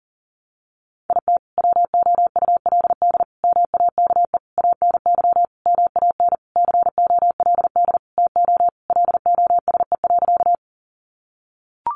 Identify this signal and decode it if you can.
Morse code, heard as a keyed tone, about 40 wpm, 700 Hz. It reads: IT WOULD MAKE ANY MAN COLD TO LOSE.
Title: Sentences from Top 1100 Words 40wpm